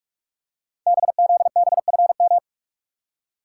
Morse code, heard as a keyed tone, about 44 wpm, 690 Hz. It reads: BZBFM
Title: Code Groups - 5 Letters - 45wpm